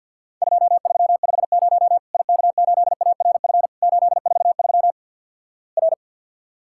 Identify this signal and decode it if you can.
Morse code, heard as a keyed tone, about 50 wpm, 700 Hz. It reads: 2350 IK8ARF 843 R
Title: Worked All Europe DX Contest - QTC Exchange with Repeats - 50wpm